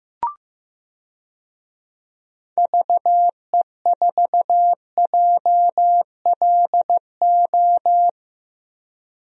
Morse code, heard as a keyed tone, about 15 wpm, 700 Hz. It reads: VE4JLO
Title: DXCC-100 Call Signs - 6 Characters - Repeated 2x - 15wpm